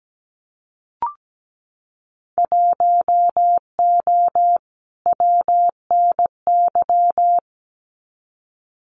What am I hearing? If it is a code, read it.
Morse code, heard as a keyed tone, about 17 wpm, 700 Hz. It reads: 1O WNY
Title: Winter Field Day Exchanges 17wpm